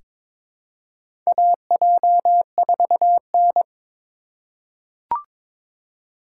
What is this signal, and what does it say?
Morse code, heard as a keyed tone, about 22 wpm, 700 Hz. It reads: AJ4N
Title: USA Call Signs - 4 Characters - 4x Character Spacing - 22wpm